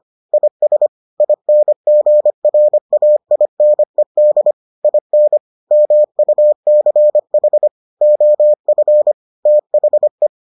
Morse code, heard as a keyed tone, about 25 wpm, 600 Hz. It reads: IS INGRAINED IN MUCH OF THE